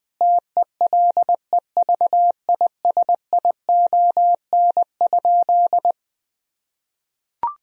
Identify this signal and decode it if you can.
Morse code, heard as a keyed tone, about 20 wpm, 700 Hz. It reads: TELEVISION?
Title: Sentences from Top 1000 Words - Review Part 4 - 20wpm